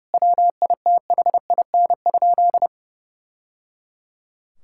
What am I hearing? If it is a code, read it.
Morse code, heard as a keyed tone, about 30 wpm, 700 Hz. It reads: WITHIN?
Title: Sentences from Top 1000 Words - Review Part 6 - 30wpm